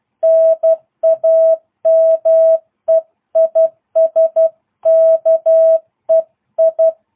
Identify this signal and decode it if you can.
Morse code, heard as a keyed tone, about 12 wpm, 650 Hz. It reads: NAME IS KEI